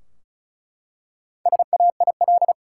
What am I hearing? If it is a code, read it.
Morse code, heard as a keyed tone, about 34 wpm, 700 Hz. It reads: SAIL
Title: Four Letter Words - 35wpm